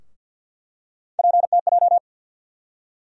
Morse code, heard as a keyed tone, about 48 wpm, 700 Hz. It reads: PTJ
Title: USA Call Sign Postfixes - 3 Letters 50wpm